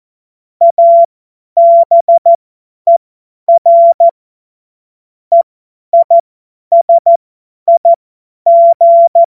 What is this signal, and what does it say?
Morse code, heard as a keyed tone, about 14 wpm, spaced as for 10 wpm, 680 Hz. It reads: ABER EISIG
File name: Morsekurs / Witze / Witz 22. August 2023 50 F70